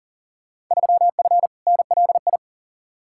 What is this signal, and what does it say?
Morse code, heard as a keyed tone, about 40 wpm, 700 Hz. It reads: 3F NLI
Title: Field Day Exchanges Encore - 40wpm